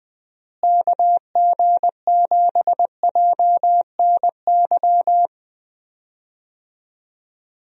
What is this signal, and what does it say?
Morse code, heard as a keyed tone, about 21 wpm, 700 Hz. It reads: KG7JNY
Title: USA Call Signs - 6 Characters - Repeated 2x - 20wpm